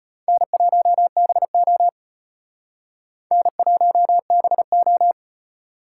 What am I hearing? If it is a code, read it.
Morse code, heard as a keyed tone, about 32 wpm, 700 Hz. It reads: N1BO N1BO